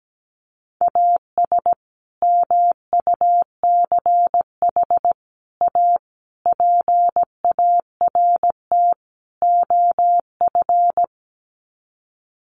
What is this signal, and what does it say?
Morse code, heard as a keyed tone, about 17 wpm, 700 Hz. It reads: AS MUCH A PART OF
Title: Sets of 5 Words Encore2 17wpm